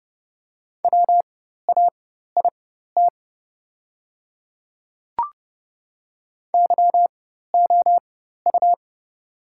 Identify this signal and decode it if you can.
Morse code, heard as a keyed tone, about 30 wpm, spaced as for 14 wpm, 700 Hz. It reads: WAIT YOU
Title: 4x Character Spacing - Top 500 Words - 30wpm